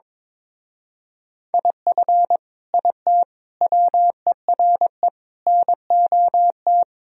Morse code, heard as a keyed tone, about 22 wpm, 700 Hz. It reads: IF IT WERE NOT
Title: Speed-Racing - Sets of 5 Words 22wpm